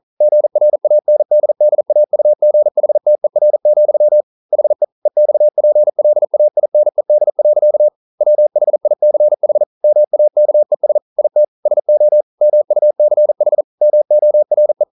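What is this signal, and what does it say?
Morse code, heard as a keyed tone, about 42 wpm, 600 Hz. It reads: GRANDDAUGHTER, HE EXPLAINED. WHICH MAKES IT SO MUCH MORE
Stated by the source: Morse code practice at 41 wpm, from CW med Gnister.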